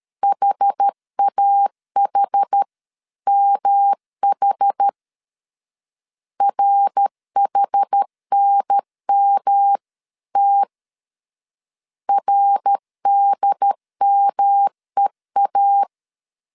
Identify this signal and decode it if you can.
Morse code, heard as a keyed tone, about 12 wpm, 800 Hz. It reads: HAHMH RHNMT RDMEA
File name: CW / DET / 1 3 1500C S B 12wpm 2828